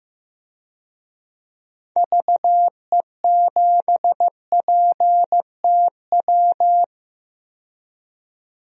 Morse code, heard as a keyed tone, about 15 wpm, 700 Hz. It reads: VE7PTW
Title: DXCC-100 Call Signs - 6 Characters - 15wpm